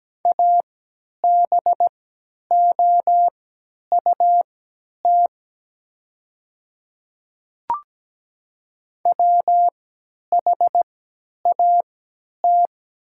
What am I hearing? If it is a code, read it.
Morse code, heard as a keyed tone, about 17 wpm, spaced as for 10 wpm, 700 Hz. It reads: ABOUT WHAT
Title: 3x Character Spacing - Top 100 Words - 17wpm